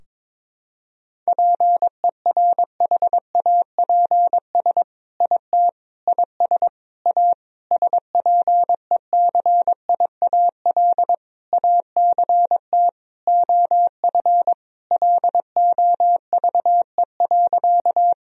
Morse code, heard as a keyed tone, about 22 wpm, 700 Hz. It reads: PERHAPS IT IS A SPECIAL ACT OF LOVE.
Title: Sentences from Top 1000 Words - Review Part 3 - 22wpm